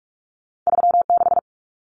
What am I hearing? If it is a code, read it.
Morse code, heard as a keyed tone, about 46 wpm, 700 Hz. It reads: 36
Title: Common Cut Numbers - 2-Digit 45wpm